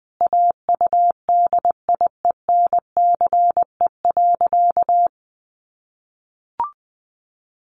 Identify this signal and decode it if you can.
Morse code, heard as a keyed tone, about 20 wpm, 700 Hz. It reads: AUDIENCE.
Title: Sentences from Top 800 Words 20wpm